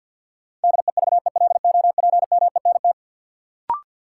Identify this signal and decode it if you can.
Morse code, heard as a keyed tone, about 46 wpm, 700 Hz. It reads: DEVELOPMENT
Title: Sets of 3 Words - New Word Drill - 50wpm